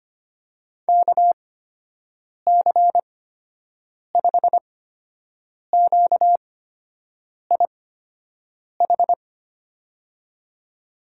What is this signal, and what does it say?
Morse code, heard as a keyed tone, about 25 wpm, spaced as for 7 wpm, 700 Hz. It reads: KC5QIH
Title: USA Call Signs - 6 Characters - 8x Character Spacing - 25wpm